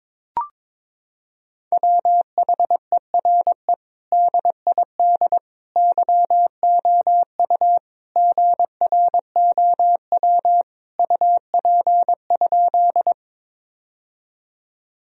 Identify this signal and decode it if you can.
Morse code, heard as a keyed tone, about 22 wpm, 700 Hz. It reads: WHERE DID YOU GROW UP?
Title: Sentences from Top 1000 Words - Review Part 1 - 22wpm